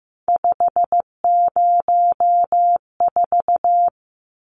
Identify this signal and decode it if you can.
Morse code, heard as a keyed tone, about 15 wpm, 700 Hz. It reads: 504
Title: Numbers - 3-Digit 15wpm